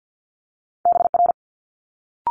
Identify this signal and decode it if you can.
Morse code, heard as a keyed tone, about 50 wpm, 700 Hz. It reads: BR
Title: Multi-letter Phonemes Quick Fire 50wpm